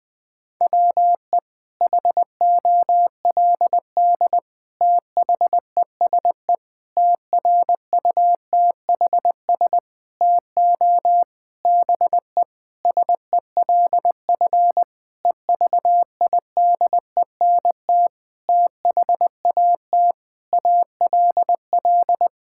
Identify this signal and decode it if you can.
Morse code, heard as a keyed tone, about 20 wpm, 700 Hz. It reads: WE HOLD THESE TRUTHS TO BE SELF EVIDENT THAT ALL
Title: Sentences with 9-Word Phrases - Spoken Beginning - 20wpm